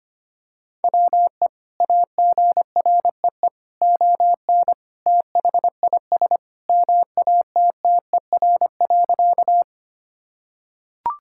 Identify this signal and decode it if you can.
Morse code, heard as a keyed tone, about 25 wpm, 700 Hz. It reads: WE AGREE ON THIS MATTER.
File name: practice / 25wpm / Sentences with 2-Word Phrases - 25wpm